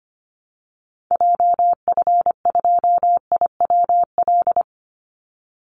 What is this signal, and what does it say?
Morse code, heard as a keyed tone, about 25 wpm, 700 Hz. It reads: JF2IWL
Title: Call Signs Repeated 2x - 25wpm